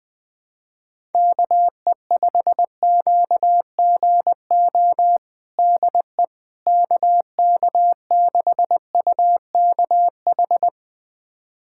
Morse code, heard as a keyed tone, about 20 wpm, 700 Hz. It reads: KE5QGO DE KK6UKH
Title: USA Callsign DE Callsign - 6 Characters - 20wpm